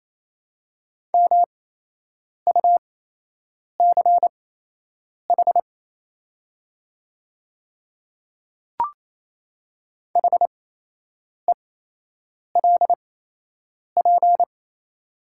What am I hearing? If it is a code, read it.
Morse code, heard as a keyed tone, about 28 wpm, spaced as for 8 wpm, 700 Hz. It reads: MUCH HELP
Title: 8x Character Spacing - Top 200 Words - 28wpm